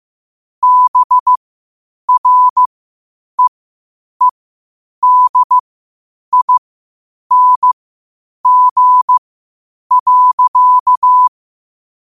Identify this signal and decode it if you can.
Morse code, heard as a keyed tone, about 15 wpm, spaced as for 8 wpm, 1000 Hz. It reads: BREEDING.
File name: qotdfiles / 8wpm